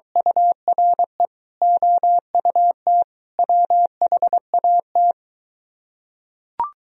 Morse code, heard as a keyed tone, about 24 wpm, 700 Hz. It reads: URE OUT WHAT